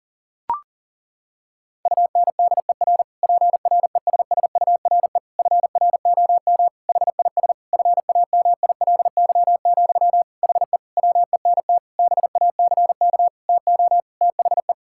Morse code, heard as a keyed tone, about 38 wpm, 700 Hz. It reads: UNDER PRESSURE FROM HIS FAMILY, HE WENT BACK TO THE
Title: Sentences from Top 1000 Words - Review Part 6 - 40wpm